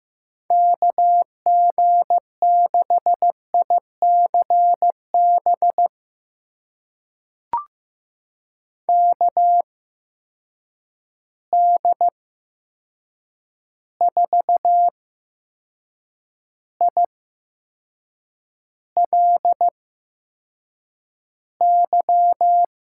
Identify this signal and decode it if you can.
Morse code, heard as a keyed tone, about 15 wpm, spaced as for 6 wpm, 700 Hz. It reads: KG6ICB KD4ILY
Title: USA Call Signs - 6 Characters - 8x Character Spacing - 15wpm